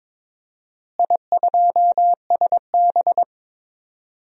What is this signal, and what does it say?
Morse code, heard as a keyed tone, about 22 wpm, 700 Hz. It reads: I2SB
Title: DXCC-100 Call Signs - 4 Characters - Repeated 3x - 22wpm